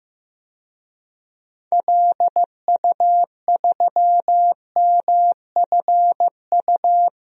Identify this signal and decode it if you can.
Morse code, heard as a keyed tone, about 15 wpm, 700 Hz. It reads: LU3MFU
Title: DXCC-100 Call Signs - 6 Characters - Repeated 2x - 15wpm